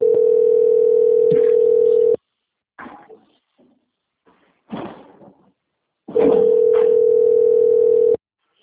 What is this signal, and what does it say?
Ringback tone (North America), 440 + 480 Hz.
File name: out- - - - - .36601